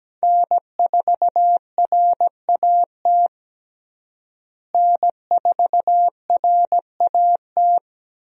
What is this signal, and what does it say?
Morse code, heard as a keyed tone, about 17 wpm, 700 Hz. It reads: N4RAT N4RAT
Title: USA Call Signs - 5 Characters - Repeated 2x - 17wpm